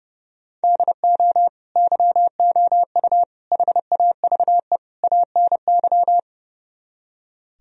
Morse code, heard as a keyed tone, about 36 wpm, 700 Hz. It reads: DO YOU HAVE ANY
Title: Speed-Racing - Sets of 4 Words 30wpm